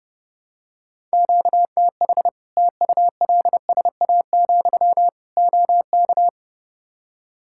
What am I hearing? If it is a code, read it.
Morse code, heard as a keyed tone, about 32 wpm, 700 Hz. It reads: QTH TULSA, OK
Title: Practice QSO - Exchange 1 - Beginning - 30wpm